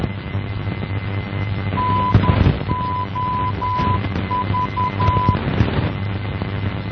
Morse code, heard as a keyed tone, about 10 wpm, 1000 Hz, with noise. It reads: NOV
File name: NOV NDB 1988